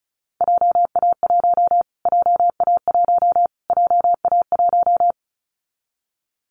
Morse code, heard as a keyed tone, about 34 wpm, 700 Hz. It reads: JA1 JA1 JA1
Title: Callsign Prefixes for Top 25 DXCC - Repeated 3x - 35wpm